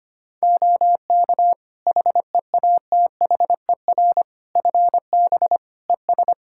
Morse code, heard as a keyed tone, about 25 wpm, 700 Hz. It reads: OK HEATHER FB ES
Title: K1USN SST QSO Practice 25wpm